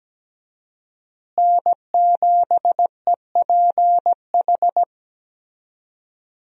Morse code, heard as a keyed tone, about 17 wpm, 700 Hz. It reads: N7EPH
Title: USA Call Signs - 5 Characters - Repeated 2x - 17wpm